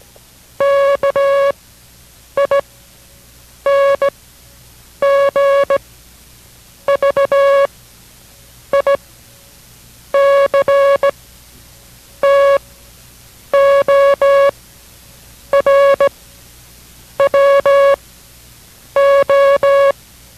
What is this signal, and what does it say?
Morse code, heard as a keyed tone, about 17 wpm, spaced as for 7 wpm, 560 Hz. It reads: KING VICTOR WO